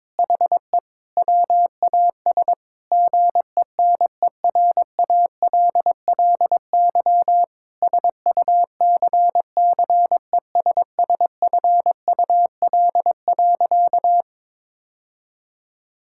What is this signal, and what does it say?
Morse code, heard as a keyed tone, about 22 wpm, 700 Hz. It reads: HE WAS GENERALLY SUCCESSFUL.
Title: Sentences from Top 1000 Words - Review Part 6 - 22wpm